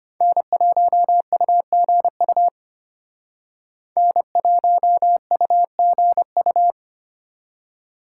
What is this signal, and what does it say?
Morse code, heard as a keyed tone, about 33 wpm, 700 Hz. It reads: N1UGU N1UGU